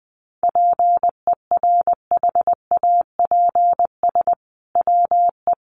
Morse code, heard as a keyed tone, about 20 wpm, 700 Hz. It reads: PERHAPS WE
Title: Sentences from Top 1500 Words 20wpm